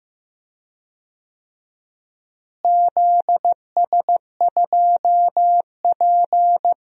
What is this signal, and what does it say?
Morse code, heard as a keyed tone, about 16 wpm, 700 Hz. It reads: ZS2P
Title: DXCC-100 Call Signs - 4 Characters - Repeated 2x - 15wpm